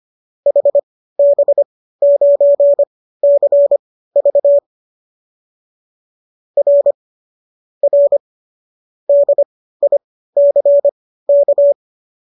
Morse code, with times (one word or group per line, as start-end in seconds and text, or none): 0.5-4.6 HB9CV
6.6-6.9 R
7.8-8.2 R
9.1-11.7 DICK